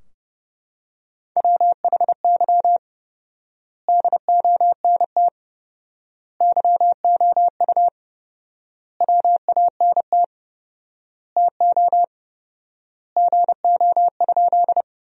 Morse code, heard as a keyed tone, about 32 wpm, 700 Hz. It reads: WHY DONT YOU WANT TO GO?
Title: 4x Word-Spacing - Sentences from Top 300 Words - 30wpm